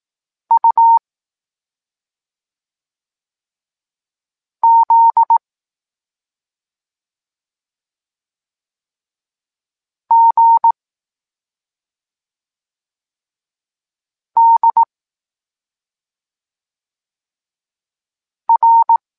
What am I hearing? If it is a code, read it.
Morse code, heard as a keyed tone, about 18 wpm, 920 Hz. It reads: U Z G D R
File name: Tekst-26 1